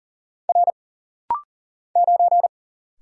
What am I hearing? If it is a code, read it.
Morse code, heard as a keyed tone, about 40 wpm, 700 Hz. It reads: R 9
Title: Single Letter-Number - Rapid-Fire 40wpm